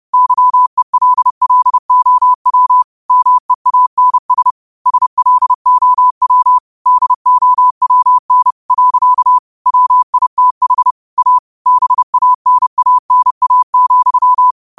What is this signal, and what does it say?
Morse code, heard as a keyed tone, about 30 wpm, 1000 Hz. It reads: YELLOW MEANS SLOW DOWN . WITH A BANANA,
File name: qotdfiles / 30wpm